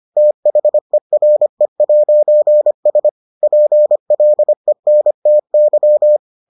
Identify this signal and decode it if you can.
Morse code, heard as a keyed tone, about 25 wpm, 600 Hz. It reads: THERE'S PLENTY